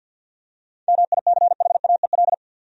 Morse code, heard as a keyed tone, about 52 wpm, 700 Hz. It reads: MICHAEL
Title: USA Common Names - 50wpm